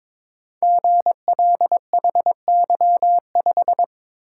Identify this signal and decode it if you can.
Morse code, heard as a keyed tone, about 22 wpm, 700 Hz. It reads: GLHY5
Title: Code Groups - 5 Letter:Numbers - 22wpm